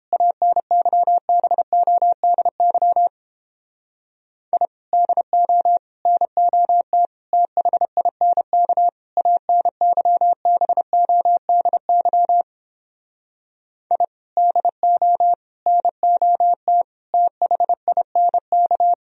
Morse code, heard as a keyed tone, about 31 wpm, 700 Hz. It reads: ANYBODY I DO NOT THINK ANYBODY I DO NOT THINK